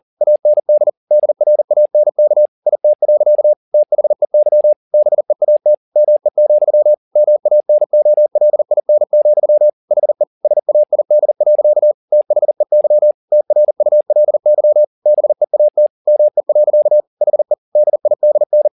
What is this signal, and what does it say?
Morse code, heard as a keyed tone, about 40 wpm, 600 Hz. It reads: AND DRANK IT. THEY BEAT ME, MANOLIN, HE SAID. THEY TRULY BEAT ME. HE DIDN
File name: the old man and the sea 2 40wpm